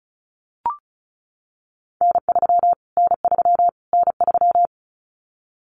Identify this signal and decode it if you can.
Morse code, heard as a keyed tone, about 34 wpm, 700 Hz. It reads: N3 N3 N3
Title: Callsign Prefixes for Top 25 DXCC - Repeated 3x - 35wpm